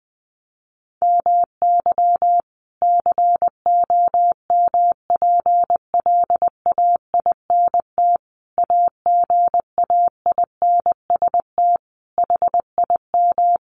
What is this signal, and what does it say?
Morse code, heard as a keyed tone, about 20 wpm, 700 Hz. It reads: MY COMPLAINT AGAINST HIM
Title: Sentences from Top 1400 Words 20wpm